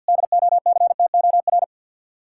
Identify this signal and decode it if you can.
Morse code, heard as a keyed tone, about 63 wpm, 700 Hz. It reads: DOCTOR